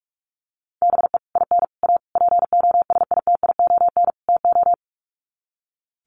Morse code, heard as a keyed tone, about 44 wpm, 700 Hz. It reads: BE IN A POSITION TO
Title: Sets of 5 Words Encore2 45wpm